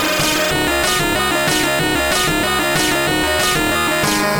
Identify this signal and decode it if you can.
There is no code.